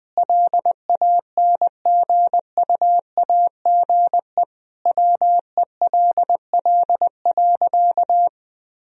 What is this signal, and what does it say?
Morse code, heard as a keyed tone, about 21 wpm, 700 Hz. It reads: LANGUAGE WELL.
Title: Sentences from Top 1000 Words - Review Part 3 - 20wpm